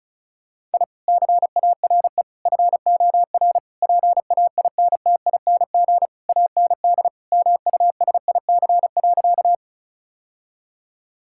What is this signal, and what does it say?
Morse code, heard as a keyed tone, about 36 wpm, 700 Hz. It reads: I CARE FOR PAINTING AND MUSIC.
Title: Sentences from Top 1000 Words - Review Part 6 - 35wpm